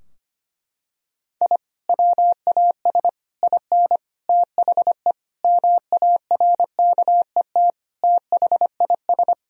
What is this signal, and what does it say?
Morse code, heard as a keyed tone, about 25 wpm, 700 Hz. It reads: I WAS IN THE MARKET THIS
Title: Sentences from Top 1000 Words - Review Part 3 - 25wpm